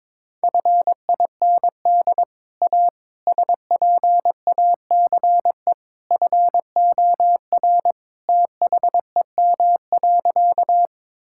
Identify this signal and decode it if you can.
Morse code, heard as a keyed tone, about 22 wpm, 700 Hz. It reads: FIND A SPACE FOR THEM.
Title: Sentences from Top 1000 Words - Review Part 5 - 22wpm